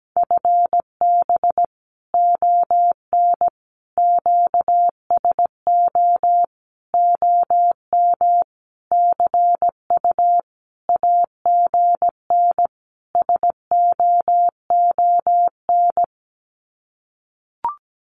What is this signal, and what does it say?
Morse code, heard as a keyed tone, about 17 wpm, 700 Hz. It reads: FB ON QSO OM CU AGN SOON
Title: Common QSO Phrases 17wpm